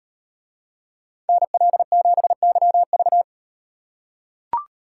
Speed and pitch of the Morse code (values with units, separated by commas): 35 wpm, 700 Hz